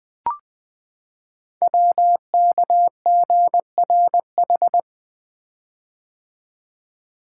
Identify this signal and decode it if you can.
Morse code, heard as a keyed tone, about 20 wpm, 700 Hz. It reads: WKGRH
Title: Code Groups - 5 Letters - 20wpm